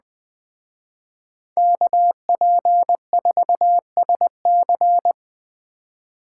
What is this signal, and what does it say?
Morse code, heard as a keyed tone, about 20 wpm, 700 Hz. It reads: KP4SC
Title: Speed Racing - USA 2x2 Call Signs 17wpm